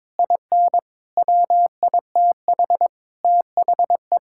Morse code, heard as a keyed tone, about 22 wpm, 700 Hz. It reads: IN WITH THE
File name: Sentences with 9-Word Phrases - 22wpm